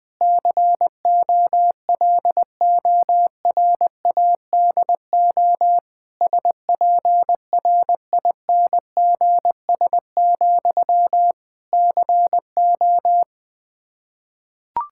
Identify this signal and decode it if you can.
Morse code, heard as a keyed tone, about 20 wpm, 700 Hz. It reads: COLORADO SPRINGS, CO
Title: Practice QSO - Exchange 1 - Beginning - 20wpm